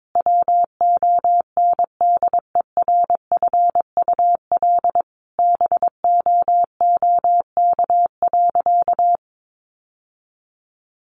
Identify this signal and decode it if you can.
Morse code, heard as a keyed tone, about 22 wpm, 700 Hz. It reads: WONDERFUL BOOK.
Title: Sentences from Top 1100 Words 22wpm